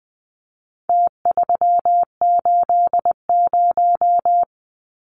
Morse code, heard as a keyed tone, about 20 wpm, 700 Hz. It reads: T380
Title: Numbers - Contest Style - 4-Digit 20wpm